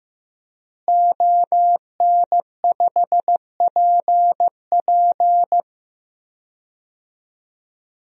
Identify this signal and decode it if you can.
Morse code, heard as a keyed tone, about 15 wpm, 700 Hz. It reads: ON5PP
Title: DXCC-100 Call Signs - 5 Characters - Repeated 2x - 15wpm